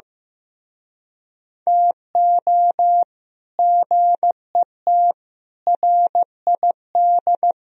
Morse code, heard as a keyed tone, about 15 wpm, 700 Hz. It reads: TO GET RID
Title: Speed-Racing - Sets of 5 Words 15wpm